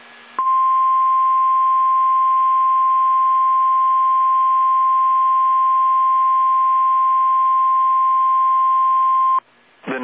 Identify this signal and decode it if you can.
Emergency Alert System (SAME) tones (the 1050 Hz attention tone). It is heard from 0.3 to 9.4 s.